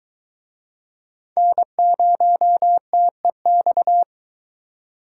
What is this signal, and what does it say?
Morse code, heard as a keyed tone, about 21 wpm, 700 Hz. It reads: N0TEX